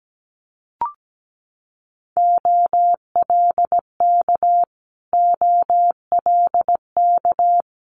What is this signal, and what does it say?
Morse code, heard as a keyed tone, about 17 wpm, 700 Hz. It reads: OLK OLK
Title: Multi-letter Phonemes Repeated 3x 17wpm